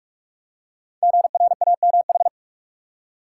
Morse code, heard as a keyed tone, about 44 wpm, 690 Hz. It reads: GRAMH
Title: Code Groups - 5 Letters - 45wpm